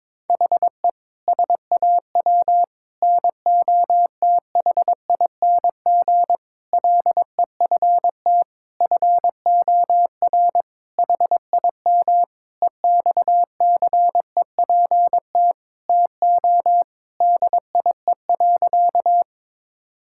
Morse code, heard as a keyed tone, about 22 wpm, 700 Hz. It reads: HE SAW NOTHING LEFT FOR HIM EXCEPT TO DIE.
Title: Sentences from Top 1600 Words 22wpm